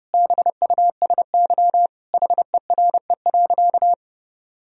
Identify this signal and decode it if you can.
Morse code, heard as a keyed tone, about 30 wpm, 700 Hz. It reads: BUSY HERE.
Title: Sentences from Top 1000 Words - Review Part 3 - 30wpm